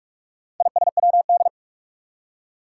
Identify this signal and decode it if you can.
Morse code, heard as a keyed tone, about 46 wpm, 700 Hz. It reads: ISWD
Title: Four Challenging Letter-Number Combinations 45wpm